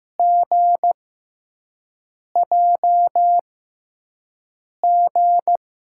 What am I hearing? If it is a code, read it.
Morse code, heard as a keyed tone, about 15 wpm, spaced as for 5 wpm, 700 Hz. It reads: GJG
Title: USA Call Signs - 6x Character Spacing - 15wpm